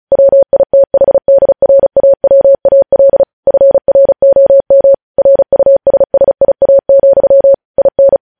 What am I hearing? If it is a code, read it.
Morse code, heard as a keyed tone, about 36 wpm, 550 Hz. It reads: WITHDRAWAL FROM RUSSIA, IN
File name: podcast / News headlines-1x-35x35